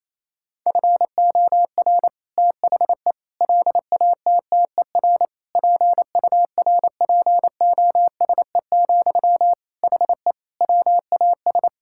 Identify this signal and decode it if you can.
Morse code, heard as a keyed tone, about 28 wpm, 700 Hz. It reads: FOR THE LATTER PURPOSE, HE WAS
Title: Sentences from Top 1600 Words 28wpm